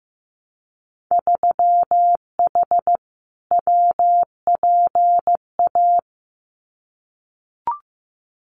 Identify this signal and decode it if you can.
Morse code, heard as a keyed tone, about 15 wpm, 700 Hz. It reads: 3H WPA
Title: Winter Field Day Exchanges Encore 15wpm